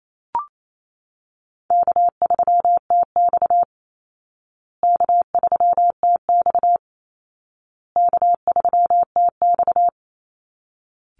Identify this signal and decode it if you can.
Morse code, heard as a keyed tone, about 28 wpm, 700 Hz. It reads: K3TX K3TX K3TX
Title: Easy Call Signs Repeated 3x - 28wpm